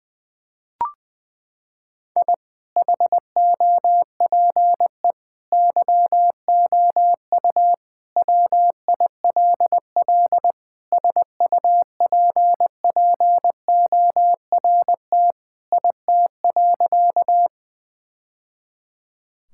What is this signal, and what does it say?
Morse code, heard as a keyed tone, about 20 wpm, 700 Hz. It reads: I HOPE YOU WILL SUPPORT IT.
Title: Sentences from Top 1000 Words - Review Part 5 - 20wpm